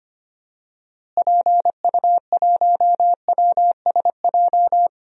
Morse code, heard as a keyed tone, about 25 wpm, 700 Hz. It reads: PU1WSJ
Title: DXCC-100 Call Signs - 6 Characters - Repeated 3x - 25wpm